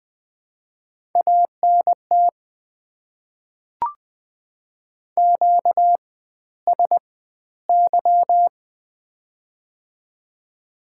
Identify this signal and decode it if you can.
Morse code, heard as a keyed tone, about 20 wpm, spaced as for 13 wpm, 700 Hz. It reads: ANT QSY
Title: Top 90 QSO Elements - 4x Character Spacing - 20wpm